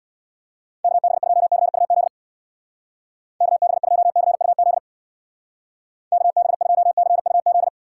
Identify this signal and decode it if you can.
Morse code, heard as a keyed tone, about 63 wpm, 700 Hz. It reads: KB2CUZ KB2CUZ KB2CUZ